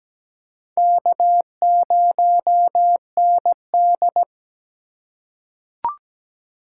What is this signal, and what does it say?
Morse code, heard as a keyed tone, about 17 wpm, 700 Hz. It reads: K0ND
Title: USA Call Signs - 4 Characters - 3x Character Spacing - 17wpm